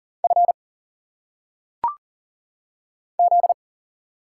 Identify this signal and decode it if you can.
Morse code, heard as a keyed tone, about 40 wpm, 700 Hz. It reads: F Z
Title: Single Letters - Phonetic Pronunciation 40wpm